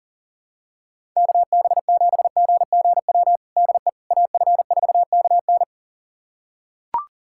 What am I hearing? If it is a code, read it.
Morse code, heard as a keyed tone, about 40 wpm, 700 Hz. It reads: KB7GGW DE AF4KN
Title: USA Callsign DE Callsign 40wpm